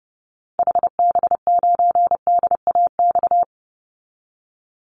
Morse code, heard as a keyed tone, about 30 wpm, 700 Hz. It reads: HB9DAX
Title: Call Signs Repeated 2x - 30wpm